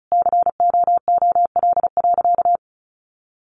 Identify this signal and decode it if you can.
Morse code, heard as a keyed tone, about 34 wpm, 700 Hz. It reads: COOL.
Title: Sentences from Top 1200 Words 35wpm